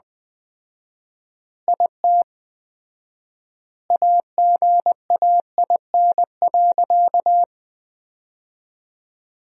Morse code, heard as a keyed tone, about 20 wpm, 700 Hz. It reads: IT AGAIN.
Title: 4x Word-Spacing - Sentences from Top 300 Words - 20wpm